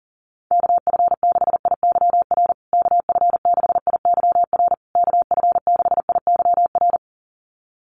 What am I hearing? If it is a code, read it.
Morse code, heard as a keyed tone, about 40 wpm, 700 Hz. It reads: KF6IYR KF6IYR KF6IYR
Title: USA Call Signs Repeated 3x 40wpm